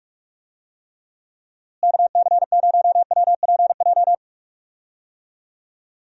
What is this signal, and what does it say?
Morse code, heard as a keyed tone, about 43 wpm, 700 Hz. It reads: KC0WPJ
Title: USA Call Signs - 6 Characters - Repeated 2x - 45wpm